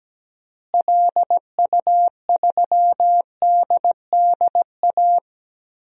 Morse code, heard as a keyed tone, about 17 wpm, 700 Hz. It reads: LU3DDA
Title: DXCC-100 Call Signs - 6 Characters - Repeated 2x - 17wpm